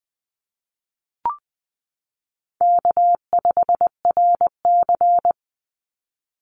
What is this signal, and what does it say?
Morse code, heard as a keyed tone, about 20 wpm, 700 Hz. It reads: K5RC
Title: Easy Call Signs Repeated 2x - 20wpm